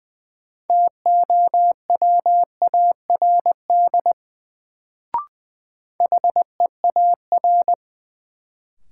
Morse code, heard as a keyed tone, about 20 wpm, 700 Hz. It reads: TOWARD HEAR
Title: Sets of 5 Words - New Word Drill - 20wpm